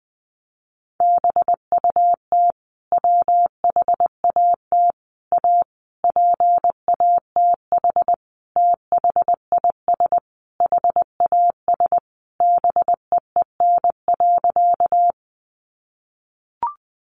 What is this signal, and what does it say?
Morse code, heard as a keyed tone, about 20 wpm, 700 Hz. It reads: BUT WHAT A PATH THIS HAS BEEN.
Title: Sentences from Top 1400 Words 20wpm